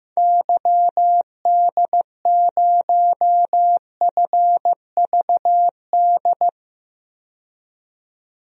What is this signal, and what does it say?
Morse code, heard as a keyed tone, about 15 wpm, 700 Hz. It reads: YD0FVD
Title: DXCC-100 Call Signs - 6 Characters - 15wpm